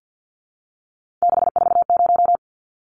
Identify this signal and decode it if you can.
Morse code, heard as a keyed tone, about 50 wpm, 700 Hz. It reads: 640
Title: Numbers - 3-Digit 50wpm